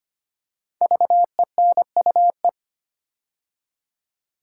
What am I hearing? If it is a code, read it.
Morse code, heard as a keyed tone, about 25 wpm, 700 Hz. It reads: VENUE
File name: 5 Letter Words - First 3 Letters Given - 25wpm